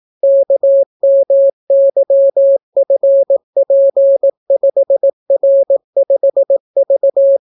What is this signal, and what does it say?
Morse code, heard as a keyed tone, about 18 wpm, 550 Hz. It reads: KMYFP5R5V